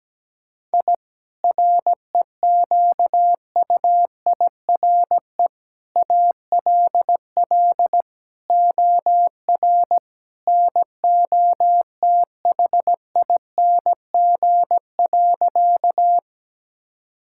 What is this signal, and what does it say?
Morse code, heard as a keyed tone, about 17 wpm, 700 Hz. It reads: I REQUIRE ALL OR NOTHING.
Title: Sentences from Top 1000 Words - Review Part 6 - 17wpm